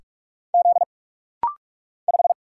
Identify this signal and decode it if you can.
Morse code, heard as a keyed tone, about 46 wpm, 700 Hz. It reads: Z 5
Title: Single Letter-Number - Rapid-Fire 45wpm